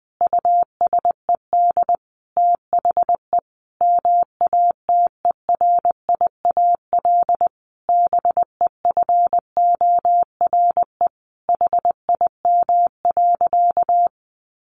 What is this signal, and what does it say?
Morse code, heard as a keyed tone, about 20 wpm, 700 Hz. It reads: USED THE MATERIAL BEFORE HIM.
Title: Sentences from Top 1400 Words 20wpm